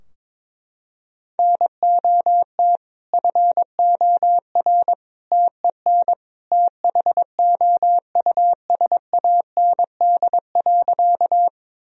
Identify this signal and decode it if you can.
Morse code, heard as a keyed tone, about 22 wpm, 700 Hz. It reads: NOT FOR TEN THOUSAND.
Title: Sentences from Top 1000 Words - Review Part 3 - 22wpm